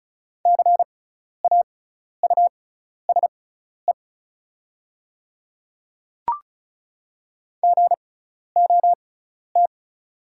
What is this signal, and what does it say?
Morse code, heard as a keyed tone, about 34 wpm, spaced as for 12 wpm, 690 Hz. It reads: CAUSE GOT
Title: 6x Character Spacing - Top 300 Words - 35wpm